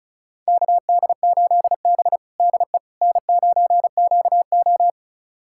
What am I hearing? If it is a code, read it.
Morse code, heard as a keyed tone, about 36 wpm, 700 Hz. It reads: KD8B DE N9QO
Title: USA Callsign DE Callsign - 4 Characters - 35wpm